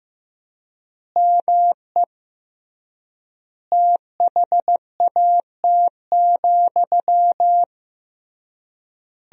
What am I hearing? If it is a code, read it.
Morse code, heard as a keyed tone, about 16 wpm, 700 Hz. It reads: ME THAT,
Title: 3x Word-Spacing - Sentences from Top 500 Words - 15wpm